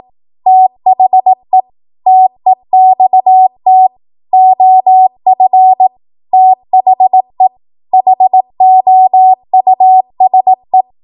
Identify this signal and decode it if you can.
Morse code, heard as a keyed tone, about 18 wpm, 750 Hz. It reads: THE TEXT OF THE HOUSE
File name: Morse / 18WPM